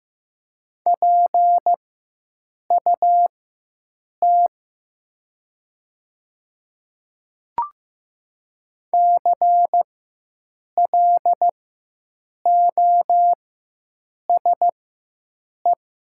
Morse code, heard as a keyed tone, about 15 wpm, spaced as for 7 wpm, 700 Hz. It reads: PUT CLOSE
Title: 4x Character Spacing - Top 300 Words - 15wpm